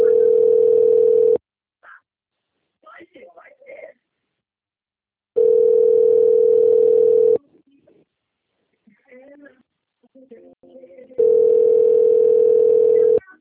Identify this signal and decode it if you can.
Ringback tone (North America), 440 + 480 Hz, with speech.
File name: out- - - - - .34237